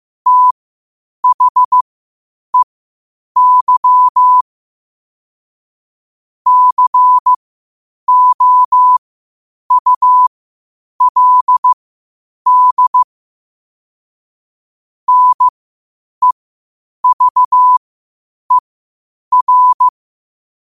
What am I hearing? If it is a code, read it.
Morse code, heard as a keyed tone, about 15 wpm, spaced as for 8 wpm, 1000 Hz. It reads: THEY COULD NEVER
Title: 8 WPM morse code quotes for Fri, 01 Mar 2024